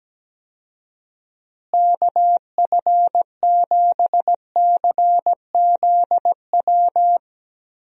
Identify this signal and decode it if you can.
Morse code, heard as a keyed tone, about 17 wpm, 700 Hz. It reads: KF7CZW
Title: USA Call Signs - 6 Characters - Repeated 3x - 17wpm